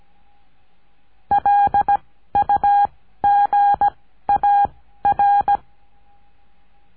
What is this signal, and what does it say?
Morse code, heard as a keyed tone, about 17 wpm, 810 Hz. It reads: L U G A R
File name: cursos / CURSO DE TELEGRAFIA / Telegrafia - Trilha 14